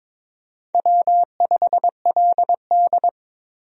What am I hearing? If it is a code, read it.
Morse code, heard as a keyed tone, about 22 wpm, 700 Hz. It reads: W5LD
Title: USA Call Signs - 4 Characters - 22wpm